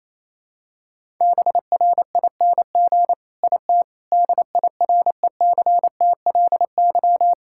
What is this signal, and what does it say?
Morse code, heard as a keyed tone, about 28 wpm, 700 Hz. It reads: BRING IT DIRECTLY
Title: Sentences from Top 1000 Words - Review Part 5 - 28wpm